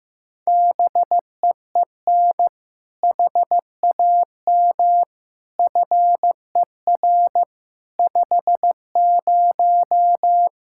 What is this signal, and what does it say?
Morse code, heard as a keyed tone, about 15 wpm, 700 Hz. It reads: BEEN HAM FER 50
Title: Practice QSO - Exchange 3 - More Info - 15wpm